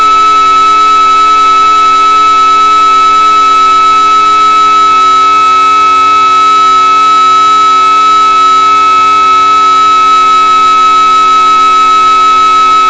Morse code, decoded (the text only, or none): none